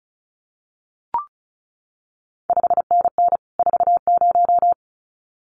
Morse code, heard as a keyed tone, about 36 wpm, 700 Hz. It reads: 5NN 40
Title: CQ WW DX Contest Exchange Encore - 35wpm